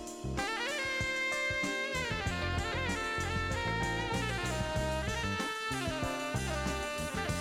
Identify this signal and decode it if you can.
Dial-up modem handshake.